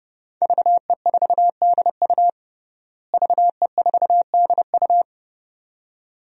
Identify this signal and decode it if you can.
Morse code, heard as a keyed tone, about 30 wpm, 700 Hz. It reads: VE4DU VE4DU
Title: DXCC-100 Call Signs - 5 Characters - Repeated 2x - 30wpm